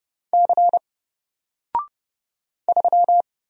Morse code, heard as a keyed tone, about 30 wpm, 700 Hz. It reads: C 3
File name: Single-Letter-Number-30wpm